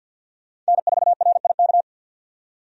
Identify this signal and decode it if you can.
Morse code, heard as a keyed tone, about 53 wpm, 700 Hz. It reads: N4RIK
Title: USA Call Signs - 5 Characters - 50wpm